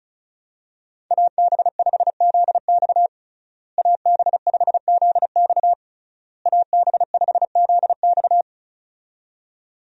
Morse code, heard as a keyed tone, about 36 wpm, 700 Hz. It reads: AB5ZX AB5ZX AB5ZX
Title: USA Call Signs - 5 Characters - Repeated 3x - 35wpm